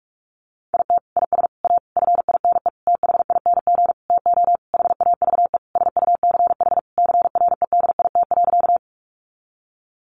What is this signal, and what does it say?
Morse code, heard as a keyed tone, about 44 wpm, 700 Hz. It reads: IT IS A FINE THING TO HAVE SUCH CREDIT.
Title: Sentences from Top 1000 Words 45wpm